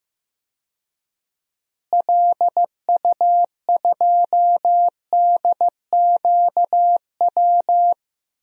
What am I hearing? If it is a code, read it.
Morse code, heard as a keyed tone, about 15 wpm, 700 Hz. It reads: LU2DQW
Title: DXCC-100 Call Signs - 6 Characters - Repeated 3x - 15wpm